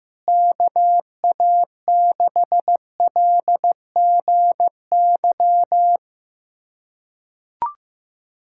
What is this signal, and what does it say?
Morse code, heard as a keyed tone, about 15 wpm, 700 Hz. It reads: KA6LGY
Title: USA Callsign DE Callsign 15wpm